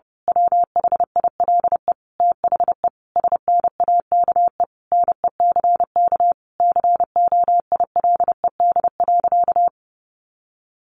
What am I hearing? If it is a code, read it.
Morse code, heard as a keyed tone, about 30 wpm, 700 Hz. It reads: WHILE THE SNAKE NECK COILED.
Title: A Sound Of Thunder 30wpm